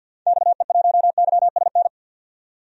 Morse code, heard as a keyed tone, about 52 wpm, 700 Hz. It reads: XE1YSN
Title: DXCC-100 Call Signs - 6 Characters - 50wpm